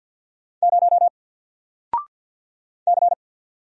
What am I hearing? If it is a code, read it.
Morse code, heard as a keyed tone, about 50 wpm, 700 Hz. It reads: 0 C
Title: Single Letter-Number 50wpm